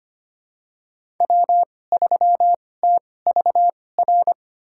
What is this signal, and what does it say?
Morse code, heard as a keyed tone, about 25 wpm, spaced as for 18 wpm, 700 Hz. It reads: W3TVR
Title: USA Call Signs - 5 Characters - 2x Character Spacing - 25wpm